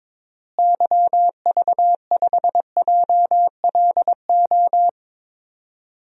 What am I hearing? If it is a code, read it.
Morse code, heard as a keyed tone, about 22 wpm, 700 Hz. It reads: YV5JLO
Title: DXCC-100 Call Signs - 6 Characters - Repeated 3x - 22wpm